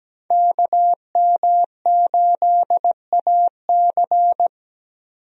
Morse code, heard as a keyed tone, about 17 wpm, 700 Hz. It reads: KM8AC
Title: USA Call Signs - 5 Characters - Repeated 3x - 17wpm